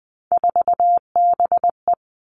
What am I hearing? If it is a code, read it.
Morse code, heard as a keyed tone, about 20 wpm, 700 Hz. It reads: 4BE
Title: Three Challenging Letter-Number Combinations 20wpm